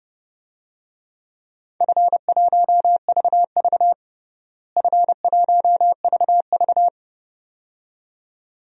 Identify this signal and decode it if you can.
Morse code, heard as a keyed tone, about 30 wpm, 700 Hz. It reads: F1VV F1VV
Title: DXCC-100 Call Signs - 4 Characters - Repeated 2x - 30wpm